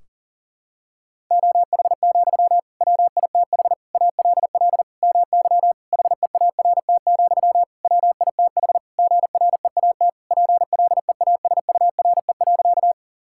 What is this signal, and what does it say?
Morse code, heard as a keyed tone, about 37 wpm, 700 Hz. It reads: OH, WITH ALL MY HEART, WITH GREAT PLEASURE.
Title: Sentences from Top 1600 Words 40wpm